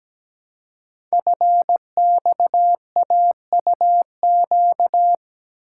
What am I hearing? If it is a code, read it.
Morse code, heard as a keyed tone, about 17 wpm, 700 Hz. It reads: FXAUQ
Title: Code Groups - 5 Letters - 17wpm